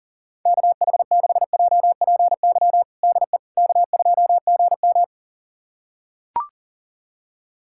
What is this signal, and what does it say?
Morse code, heard as a keyed tone, about 40 wpm, 700 Hz. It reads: KH6JPY DE K2GM
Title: USA Callsign DE Callsign 40wpm